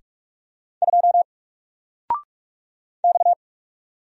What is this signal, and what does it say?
Morse code, heard as a keyed tone, about 44 wpm, 700 Hz. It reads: 2 X
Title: Single Character (Letter:Number:Punctuation:Prosign) Alt 45wpm